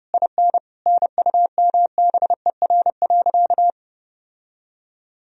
Morse code, heard as a keyed tone, about 30 wpm, 700 Hz. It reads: IN NUMBER.
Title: Sentences from Top 1000 Words - Review Part 4 - 30wpm